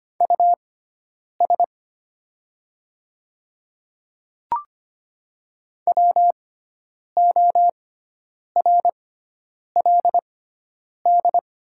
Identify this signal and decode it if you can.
Morse code, heard as a keyed tone, about 25 wpm, spaced as for 9 wpm, 690 Hz. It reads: US WORLD
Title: 6x Character Spacing - Top 200 Words - 25wpm